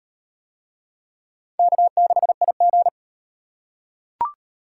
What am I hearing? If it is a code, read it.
Morse code, heard as a keyed tone, about 34 wpm, 700 Hz. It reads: K6IG